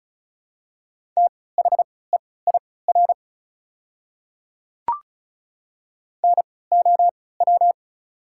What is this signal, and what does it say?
Morse code, heard as a keyed tone, about 36 wpm, spaced as for 20 wpm, 700 Hz. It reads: THEIR NOW
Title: 3x Character Spacing - Top 100 Words - 35wpm